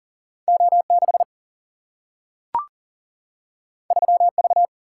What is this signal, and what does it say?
Morse code, heard as a keyed tone, about 40 wpm, 700 Hz. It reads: O 6 3 V
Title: Single-Letter Single-Number 40wpm